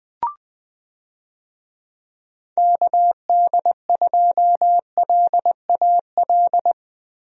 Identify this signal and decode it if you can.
Morse code, heard as a keyed tone, about 20 wpm, 700 Hz. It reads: KD2LAL
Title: USA Call Signs - 6 Characters - Repeated 2x - 20wpm